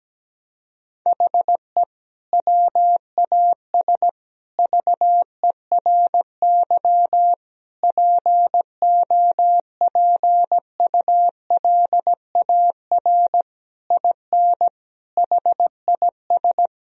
Morse code, heard as a keyed tone, about 17 wpm, 700 Hz. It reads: HE WAS VERY POPULAR IN HIS
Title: Sentences from Top 1000 Words - Review Part 3 - 17wpm